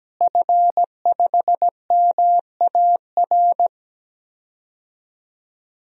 Morse code, heard as a keyed tone, about 17 wpm, 700 Hz. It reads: F5MAR
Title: DXCC-100 Call Signs - 5 Characters - Repeated 2x - 17wpm